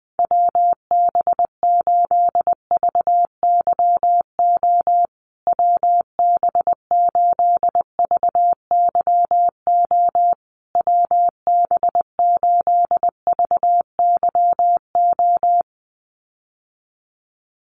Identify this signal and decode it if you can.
Morse code, heard as a keyed tone, about 20 wpm, 700 Hz. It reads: WB8VYO WB8VYO WB8VYO
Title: USA Call Signs Repeated 3x 20wpm